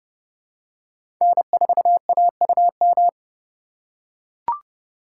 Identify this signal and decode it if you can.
Morse code, heard as a keyed tone, about 30 wpm, 700 Hz. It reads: N4AUM
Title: USA Call Signs - 4x Character Spacing - 30wpm